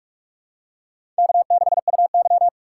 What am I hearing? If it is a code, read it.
Morse code, heard as a keyed tone, about 46 wpm, 700 Hz. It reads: K6UY
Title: USA Call Signs - 4 Characters - 6x Character Spacing - 45wpm